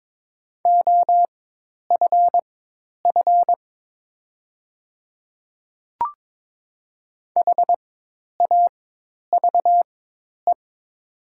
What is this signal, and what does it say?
Morse code, heard as a keyed tone, about 22 wpm, spaced as for 10 wpm, 700 Hz. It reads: OFF HAVE
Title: 4x Character Spacing - Top 200 Words - 22wpm